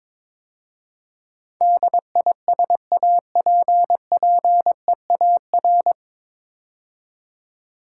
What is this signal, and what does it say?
Morse code, heard as a keyed tone, about 22 wpm, 700 Hz. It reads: DISAPPEAR
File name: 9 Letter Words - First 3 Letters Given - 22wpm